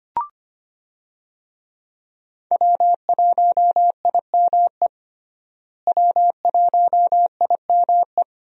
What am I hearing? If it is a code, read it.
Morse code, heard as a keyed tone, about 25 wpm, 700 Hz. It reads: W1IME W1IME
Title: USA Call Signs - 5 Characters - Repeated 2x - 25wpm